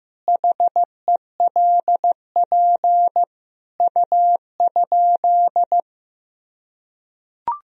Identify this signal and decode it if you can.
Morse code, heard as a keyed tone, about 15 wpm, 700 Hz. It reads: HELP U?
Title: Sentences from Top 200 Words and Abbreviations 15wpm